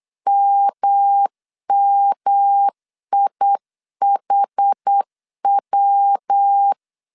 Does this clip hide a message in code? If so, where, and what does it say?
Yes: Morse code from 0.3 to 6.7 s, heard as a keyed tone, about 8 wpm, 800 Hz. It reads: MMIHW